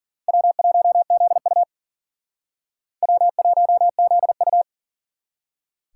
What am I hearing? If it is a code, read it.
Morse code, heard as a keyed tone, about 53 wpm, 700 Hz. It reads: W1ZU W1ZU